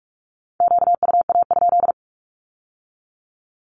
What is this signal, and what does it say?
Morse code, heard as a keyed tone, about 46 wpm, 700 Hz. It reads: QUA?
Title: Q-Codes with Spoken Meaning - 45wpm